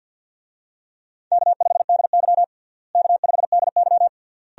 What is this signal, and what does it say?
Morse code, heard as a keyed tone, about 46 wpm, 700 Hz. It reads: K5DY K5DY
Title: USA Call Signs - 4 Characters - Repeated 3x - 50wpm